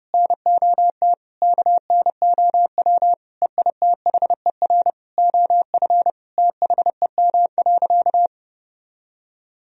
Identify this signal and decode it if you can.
Morse code, heard as a keyed tone, about 30 wpm, 700 Hz. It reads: NOT KNOW EITHER OF THEM.
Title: Sentences from Top 1000 Words - Review Part 4 - 30wpm